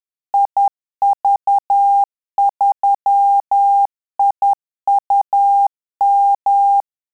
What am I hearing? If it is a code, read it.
Morse code, heard as a keyed tone, about 10 wpm, 800 Hz. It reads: IV3IUM
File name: Telegraphy-group / QSO CW 50 CPM